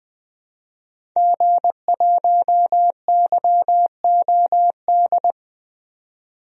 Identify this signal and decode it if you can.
Morse code, heard as a keyed tone, about 20 wpm, 700 Hz. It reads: G1YOD
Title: DXCC-100 Call Signs - 5 Characters - 20wpm